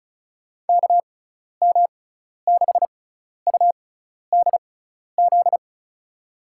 Morse code, heard as a keyed tone, about 34 wpm, spaced as for 12 wpm, 700 Hz. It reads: KM6UDZ
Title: USA Call Signs - 6x Character Spacing - 35wpm